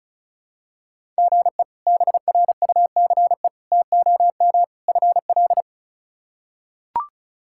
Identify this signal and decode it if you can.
Morse code, heard as a keyed tone, about 36 wpm, 700 Hz. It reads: GE BRUCE TOM FL
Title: K1USN SST QSO Practice - 35wpm